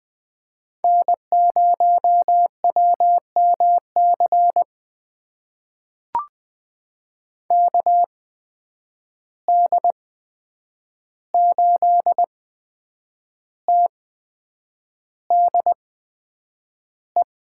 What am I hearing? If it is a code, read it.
Morse code, heard as a keyed tone, about 20 wpm, spaced as for 8 wpm, 700 Hz. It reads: N0WMC KD8TDE
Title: USA Call Signs - 8x Character Spacing - 20wpm